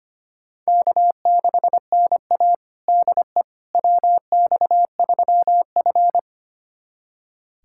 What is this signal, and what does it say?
Morse code, heard as a keyed tone, about 25 wpm, 700 Hz. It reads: K6NA DE WX3F
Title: USA Callsign DE Callsign - 4 Characters - 25wpm